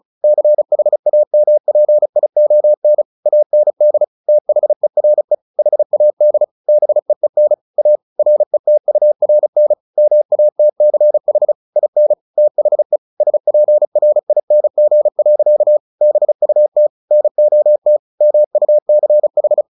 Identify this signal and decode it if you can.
Morse code, heard as a keyed tone, about 34 wpm, 600 Hz. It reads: CHAMPION AND THERE HAD BEEN A RETURN MATCH IN THE SPRING. BUT NOT MUCH
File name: the old man and the sea 2 35wpm